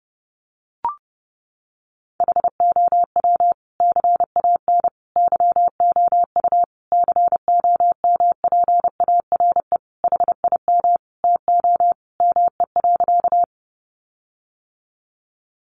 Morse code, heard as a keyed tone, about 30 wpm, 700 Hz. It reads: HOW CAN YOU COMPARE HIM TO ME.
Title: Sentences from Top 1300 Words 30wpm